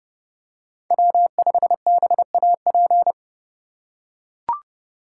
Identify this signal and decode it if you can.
Morse code, heard as a keyed tone, about 30 wpm, 700 Hz. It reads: W5BAP
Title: USA Call Signs - 5 Characters - 4x Character Spacing - 30wpm